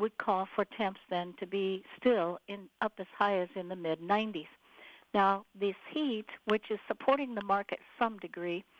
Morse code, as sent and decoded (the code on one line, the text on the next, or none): none